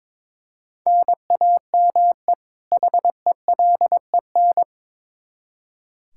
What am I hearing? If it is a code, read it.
Morse code, heard as a keyed tone, about 22 wpm, 700 Hz. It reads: NAME HELEN
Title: Practice QSO - Exchange 1 - Beginning - 22wpm